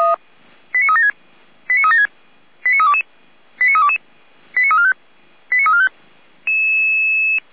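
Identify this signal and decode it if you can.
Five-tone selective call (ZVEI), heard as quick runs of five tones, tones of about 80 ms. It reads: 89376 89376 89322 89322 89345 89345
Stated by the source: ZVEI five-tone selective call received by SDR, code 89340.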